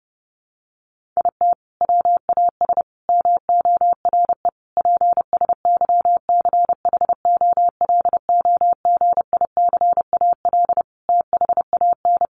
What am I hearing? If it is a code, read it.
Morse code, heard as a keyed tone, about 30 wpm, 700 Hz. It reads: IT WAS MORE PSYCHOLOGICAL THAN
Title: Sentences from Top 1400 Words 30wpm